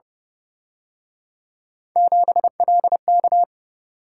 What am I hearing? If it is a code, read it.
Morse code, heard as a keyed tone, about 30 wpm, 700 Hz. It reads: 7LK
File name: Speed-Racing - Three Letter-Number Combinations - 25wpm